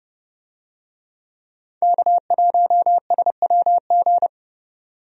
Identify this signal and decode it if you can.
Morse code, heard as a keyed tone, about 30 wpm, 700 Hz. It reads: K1SWG
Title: USA Call Signs - 5 Characters - 30wpm